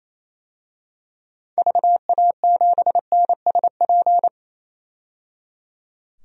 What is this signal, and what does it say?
Morse code, heard as a keyed tone, about 28 wpm, 700 Hz. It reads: VA7NSP
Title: DXCC-100 Call Signs - 6 Characters - 28wpm